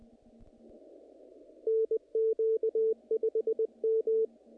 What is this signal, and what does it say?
Morse code, heard as a keyed tone, about 20 wpm, 440 Hz. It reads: NQ5M
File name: contests / naqp-cw-k5tr-op-k5pi- -2000